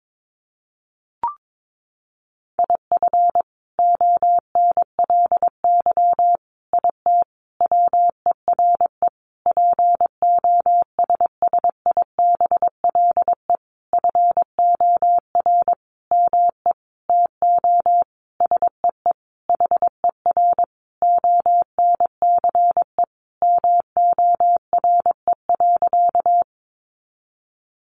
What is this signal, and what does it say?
Morse code, heard as a keyed tone, about 22 wpm, 700 Hz. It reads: IF ONLY IT WERE POSSIBLE FOR ME TO SEE HER ONCE MORE.
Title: Sentences from Top 500 Words 22wpm